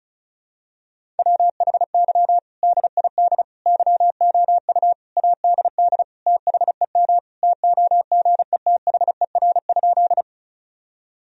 Morse code, heard as a keyed tone, about 34 wpm, 700 Hz. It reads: WHY DID YOU ADD THEM TOGETHER?
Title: Sentences from Top 1000 Words - Review Part 1 - 35wpm